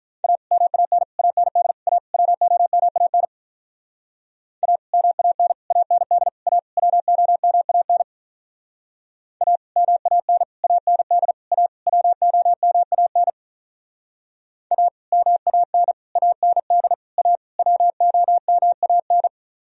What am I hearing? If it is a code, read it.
Morse code, heard as a keyed tone, about 44 wpm, 700 Hz. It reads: A MAN AND A WOMAN A MAN AND A WOMAN A MAN AND A WOMAN A MAN AND A WOMAN